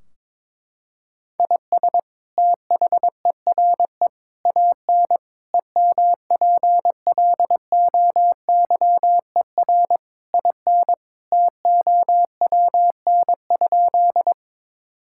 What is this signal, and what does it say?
Morse code, heard as a keyed tone, about 22 wpm, 700 Hz. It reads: IS THERE AN EMPLOYER IN TOWN?
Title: Sentences from Top 1600 Words 22wpm